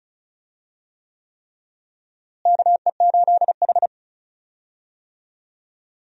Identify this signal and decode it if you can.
Morse code, heard as a keyed tone, about 36 wpm, 700 Hz. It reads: KE8H
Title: USA Call Signs - 4 Characters - Repeated 3x - 35wpm